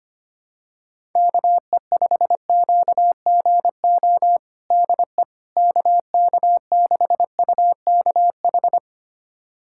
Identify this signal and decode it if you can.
Morse code, heard as a keyed tone, about 25 wpm, 700 Hz. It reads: KE5QGO DE KK6UKH
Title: USA Callsign DE Callsign - 6 Characters - 25wpm